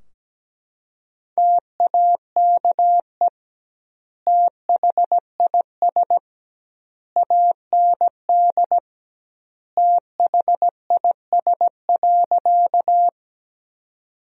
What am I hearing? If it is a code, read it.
Morse code, heard as a keyed tone, about 17 wpm, 700 Hz. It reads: TAKE THIS AND THIS.
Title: 2x Word-Spacing - Sentences from Top 200 Words - 17wpm